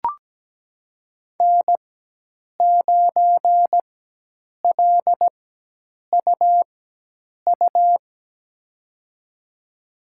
Morse code, heard as a keyed tone, about 17 wpm, spaced as for 8 wpm, 700 Hz. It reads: N9LUU
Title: USA Call Signs - 5 Characters - 4x Character Spacing - 17wpm